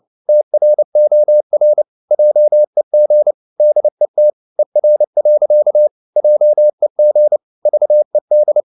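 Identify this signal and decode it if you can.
Morse code, heard as a keyed tone, about 29 wpm, 600 Hz. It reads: TROR JEG DET ER. JEG VED
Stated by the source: Fast (QRQ) Morse at 29 wpm, from CW med Gnister.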